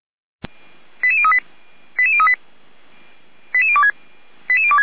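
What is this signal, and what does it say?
Five-tone selective call (ZVEI), heard as quick runs of five tones, tones of about 70 ms. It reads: 88038 88038 88026 88026
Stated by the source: ZVEI five-tone call 88355 received on SDR.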